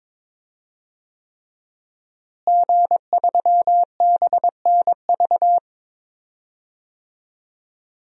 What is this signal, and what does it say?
Morse code, heard as a keyed tone, about 22 wpm, 700 Hz. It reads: G3BNV
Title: DXCC-100 Call Signs - 5 Characters - Repeated 3x - 22wpm